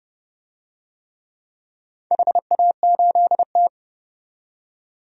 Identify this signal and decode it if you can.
Morse code, heard as a keyed tone, about 31 wpm, 700 Hz. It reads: HA8T
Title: DXCC-100 Call Signs - 4 Characters - 30wpm